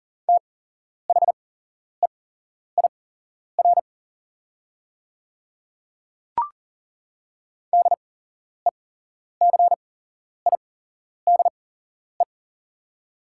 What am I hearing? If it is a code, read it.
Morse code, heard as a keyed tone, about 40 wpm, spaced as for 11 wpm, 700 Hz. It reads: THEIR DECIDE
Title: 8x Character Spacing - Top 500 Words - 40wpm